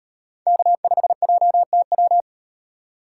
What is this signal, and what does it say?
Morse code, heard as a keyed tone, about 38 wpm, 700 Hz. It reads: K5JTW